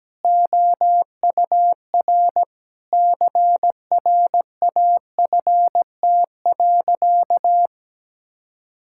Morse code, heard as a keyed tone, about 17 wpm, 700 Hz. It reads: OUR CRAFT.
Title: Sentences from Top 1000 Words - Review Part 6 - 17wpm